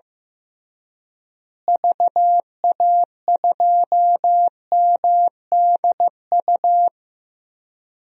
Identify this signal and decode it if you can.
Morse code, heard as a keyed tone, about 16 wpm, 700 Hz. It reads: VA2MDU
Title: DXCC-100 Call Signs - 6 Characters - Repeated 3x - 15wpm